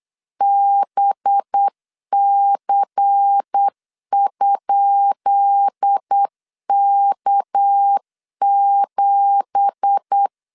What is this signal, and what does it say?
Morse code, heard as a keyed tone, about 8 wpm, 800 Hz. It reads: BC?K7
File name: LEZIONI / 10 SIGNS / 10B INT 8wpm 3534